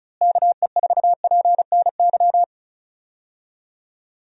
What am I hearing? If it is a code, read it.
Morse code, heard as a keyed tone, about 34 wpm, 700 Hz. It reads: KE4PNY
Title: USA Call Signs - 6 Characters - Repeated 3x - 35wpm